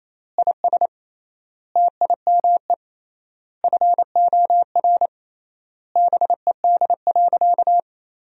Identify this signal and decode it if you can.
Morse code, heard as a keyed tone, about 28 wpm, 700 Hz. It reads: IS TIME FOR BED.
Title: 3x Word-Spacing - Sentences from Top 500 Words - 28wpm